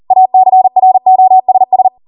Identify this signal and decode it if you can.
Morse code, heard as a keyed tone, about 40 wpm, 750 Hz. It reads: ACROSS